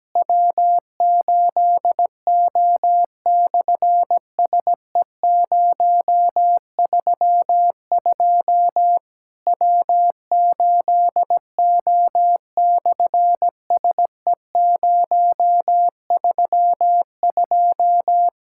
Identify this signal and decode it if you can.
Morse code, heard as a keyed tone, about 17 wpm, 700 Hz. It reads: W8O/SE032 W8O/SE032
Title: SOTA QSO Practice 17wpm